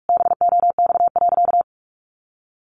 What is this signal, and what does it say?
Morse code, heard as a keyed tone, about 44 wpm, 700 Hz. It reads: BOX.
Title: Sentences from Top 700 Words 45wpm